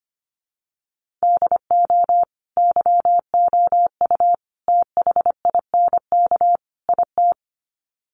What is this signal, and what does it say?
Morse code, heard as a keyed tone, about 25 wpm, 700 Hz. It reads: DO YOU THINK IT
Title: Sets of 4 Words 25wpm